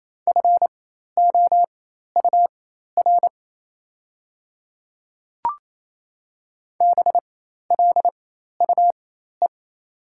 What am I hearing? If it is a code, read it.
Morse code, heard as a keyed tone, about 28 wpm, spaced as for 13 wpm, 700 Hz. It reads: FOUR BLUE
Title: 4x Character Spacing - Top 500 Words - 28wpm